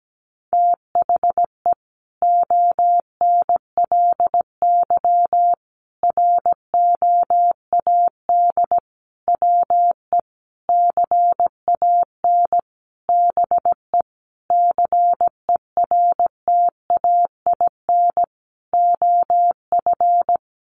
Morse code, heard as a keyed tone, about 17 wpm, 700 Hz. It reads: THE ONLY ROAD WE CAN BE CERTAIN OF
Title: Sentences from Top 1100 Words 17wpm